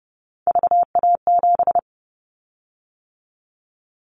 Morse code, heard as a keyed tone, about 30 wpm, 700 Hz. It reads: VA7
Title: Callsign Prefixes for Top 50 DXCC - 30wpm